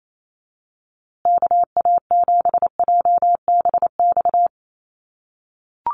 Morse code, heard as a keyed tone, about 28 wpm, 700 Hz. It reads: KA7JBX
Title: USA Call Signs Repeated 3x 28wpm